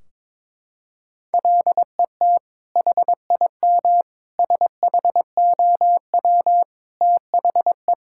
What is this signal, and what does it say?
Morse code, heard as a keyed tone, about 22 wpm, 700 Hz. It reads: LET HIM SHOW THE
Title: Sentences from Top 1600 Words 22wpm